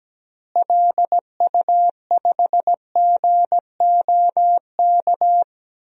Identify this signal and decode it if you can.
Morse code, heard as a keyed tone, about 17 wpm, 700 Hz. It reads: LU5GOK
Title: DXCC-100 Call Signs - 6 Characters - Repeated 3x - 17wpm